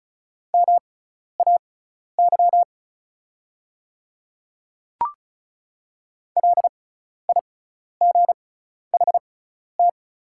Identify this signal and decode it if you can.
Morse code, heard as a keyed tone, about 36 wpm, spaced as for 12 wpm, 700 Hz. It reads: MAY LIGHT
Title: 6x Character Spacing - Top 500 Words - 35wpm